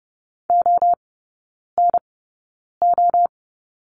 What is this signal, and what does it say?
Morse code, heard as a keyed tone, about 30 wpm, spaced as for 9 wpm, 700 Hz. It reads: ONO
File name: Lesson 17e - Letter u - Three Characters 30wpm